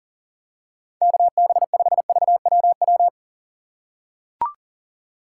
Morse code, heard as a keyed tone, about 40 wpm, 700 Hz. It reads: KB5VWW
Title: USA Call Signs - 6 Characters - 6x Character Spacing - 40wpm